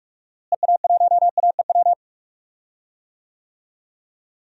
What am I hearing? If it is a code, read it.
Morse code, heard as a keyed tone, about 42 wpm, 700 Hz. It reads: EA1AEW
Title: DXCC-100 Call Signs - 6 Characters - Repeated 3x - 45wpm